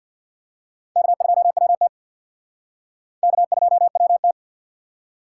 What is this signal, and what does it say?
Morse code, heard as a keyed tone, about 60 wpm, 700 Hz. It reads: K2WT K2WT
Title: Speed Racing - USA 1x2 Call Signs 50wpm